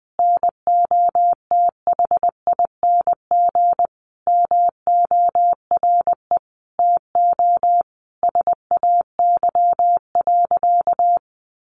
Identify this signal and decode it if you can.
Morse code, heard as a keyed tone, about 20 wpm, 700 Hz. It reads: NOTHING MORE TO SAY.
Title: Sentences from Top 500 Words 20wpm